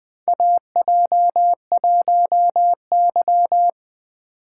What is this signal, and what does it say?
Morse code, heard as a keyed tone, about 20 wpm, 700 Hz. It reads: AJ1Y
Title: USA Call Signs - 4 Characters - 3x Character Spacing - 20wpm